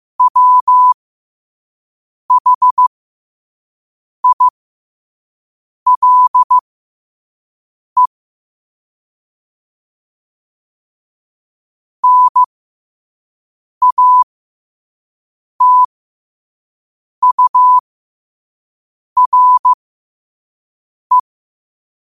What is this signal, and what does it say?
Morse code, heard as a keyed tone, about 15 wpm, spaced as for 5 wpm, 1000 Hz. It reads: WHILE NATURE
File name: qotdfiles / 5wpm